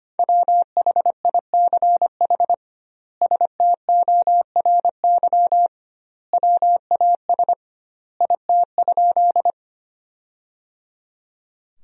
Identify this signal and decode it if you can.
Morse code, heard as a keyed tone, about 25 wpm, 700 Hz. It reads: WHICH STORY WAS IT?
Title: 2x Word-Spacing - Sentences from Top 300 Words - 25wpm